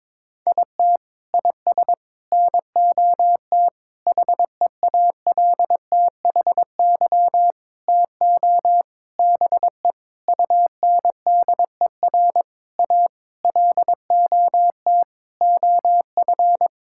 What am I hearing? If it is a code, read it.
Morse code, heard as a keyed tone, about 21 wpm, 700 Hz. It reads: IT IS NOT HEALTHY TO BE UNDER A LOT OF
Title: Sentences from Top 1000 Words - Review Part 6 - 22wpm